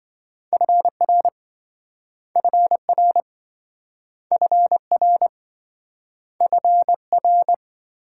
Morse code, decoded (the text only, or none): FR FR FR FR